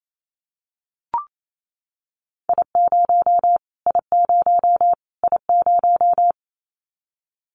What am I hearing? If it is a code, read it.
Morse code, heard as a keyed tone, about 28 wpm, 700 Hz. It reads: I0 I0 I0
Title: Callsign Prefixes for Top 25 DXCC - Repeated 3x - 28wpm